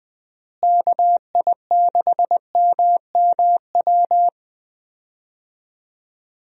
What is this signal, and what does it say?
Morse code, heard as a keyed tone, about 21 wpm, 700 Hz. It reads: KI6MMW
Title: USA Call Signs - 6 Characters - Repeated 2x - 20wpm